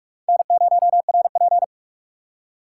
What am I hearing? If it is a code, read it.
Morse code, heard as a keyed tone, about 46 wpm, 700 Hz. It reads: N0RP
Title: USA Call Signs - 4 Characters - 6x Character Spacing - 45wpm